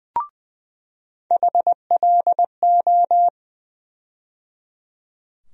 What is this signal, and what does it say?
Morse code, heard as a keyed tone, about 20 wpm, 700 Hz. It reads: HLO
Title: USA Call Sign Postfixes - 3 Letters 20wpm